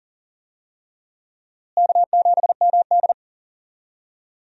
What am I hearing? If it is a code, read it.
Morse code, heard as a keyed tone, about 40 wpm, 700 Hz. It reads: K7MD
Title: USA Call Signs - 4 Characters - Repeated 2x - 40wpm